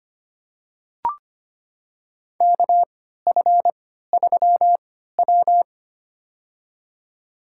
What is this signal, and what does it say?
Morse code, heard as a keyed tone, about 25 wpm, spaced as for 14 wpm, 700 Hz. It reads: KF3W
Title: USA Call Signs - 4 Characters - 3x Character Spacing - 25wpm